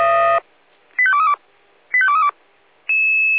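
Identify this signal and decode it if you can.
Five-tone selective call (ZVEI), heard as quick runs of five tones, tones of about 70 ms. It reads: 85232 85232
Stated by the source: ZVEI five-tone selective call received on an SDR.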